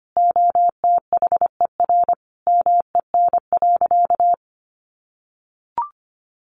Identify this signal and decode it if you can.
Morse code, heard as a keyed tone, about 25 wpm, 700 Hz. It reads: OTHER MEN.
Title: Sentences from Top 1100 Words 25wpm